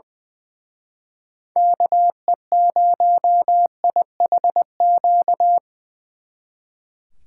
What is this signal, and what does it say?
Morse code, heard as a keyed tone, about 24 wpm, 700 Hz. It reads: KE0IHQ
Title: Speed Racing - USA 2x3 Call Signs 20wpm